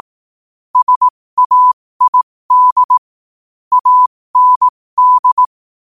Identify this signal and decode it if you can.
Morse code, heard as a keyed tone, about 18 wpm, spaced as for 15 wpm, 1000 Hz. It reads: SAID AND